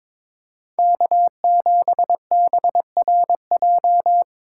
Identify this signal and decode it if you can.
Morse code, heard as a keyed tone, about 22 wpm, 700 Hz. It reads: K7BRJ
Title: USA Call Signs - 5 Characters - 6x Character Spacing - 22wpm